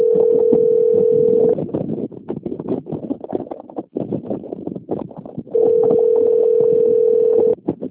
Ringback tone (North America), 440 + 480 Hz.